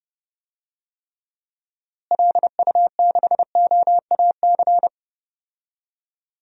Morse code, heard as a keyed tone, about 30 wpm, 700 Hz. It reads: LU6OAC
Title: DXCC-100 Call Signs - 6 Characters - Repeated 2x - 30wpm